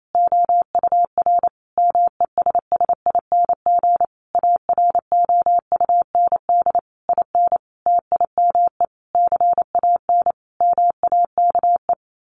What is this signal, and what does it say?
Morse code, heard as a keyed tone, about 28 wpm, 700 Hz. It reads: OUR MESSING AROUND IN TIME CAN MAKE
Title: A Sound Of Thunder 28wpm